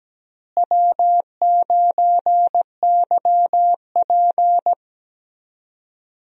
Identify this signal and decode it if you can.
Morse code, heard as a keyed tone, about 17 wpm, 700 Hz. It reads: W9YP
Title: USA Call Signs - 4 Characters - Repeated 3x - 17wpm